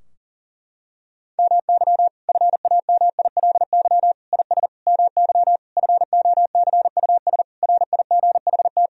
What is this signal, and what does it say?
Morse code, heard as a keyed tone, about 40 wpm, 700 Hz. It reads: MY FAMILY IS MY FOCUS RIGHT
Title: Sentences from Top 1000 Words - Review Part 4 - 40wpm